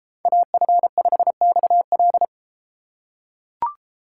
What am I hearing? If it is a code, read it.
Morse code, heard as a keyed tone, about 32 wpm, 700 Hz. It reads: AF5XL